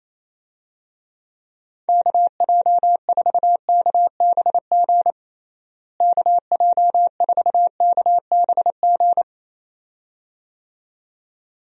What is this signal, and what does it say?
Morse code, heard as a keyed tone, about 28 wpm, 700 Hz. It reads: KJ4KBG KJ4KBG
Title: USA Call Signs - 6 Characters - Repeated 2x - 28wpm